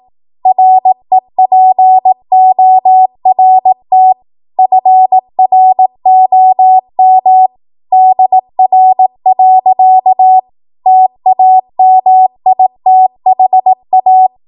Morse code, heard as a keyed tone, about 18 wpm, 750 Hz. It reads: REPORT FROM DR. TAMITHA
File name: Morse / 18WPM